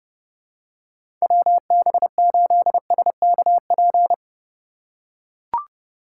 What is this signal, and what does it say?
Morse code, heard as a keyed tone, about 30 wpm, 700 Hz. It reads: WB8SKP
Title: K1USN SST QSO Practice - 30wpm